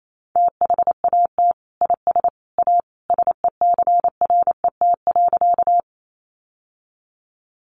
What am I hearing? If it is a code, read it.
Morse code, heard as a keyed tone, about 28 wpm, 700 Hz. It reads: THAT IS A SECRET.
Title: Sentences from Top 1200 Words 28wpm